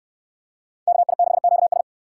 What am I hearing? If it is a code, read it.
Morse code, heard as a keyed tone, about 68 wpm, 700 Hz. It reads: KI6QS